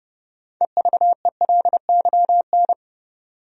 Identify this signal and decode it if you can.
Morse code, heard as a keyed tone, about 30 wpm, 700 Hz. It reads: EVELYN
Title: USA Common Names - 30wpm